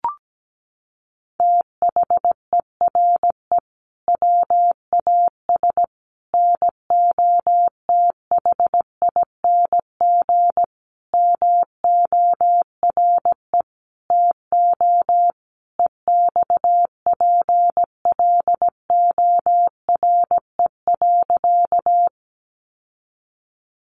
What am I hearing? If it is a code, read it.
Morse code, heard as a keyed tone, about 17 wpm, 700 Hz. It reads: THERE WAS NOTHING MORE TO EXPLORE.
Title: Sentences from Top 1100 Words 17wpm